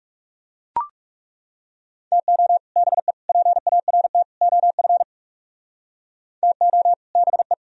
Morse code, heard as a keyed tone, about 39 wpm, 700 Hz. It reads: TO BE PART OF TO BE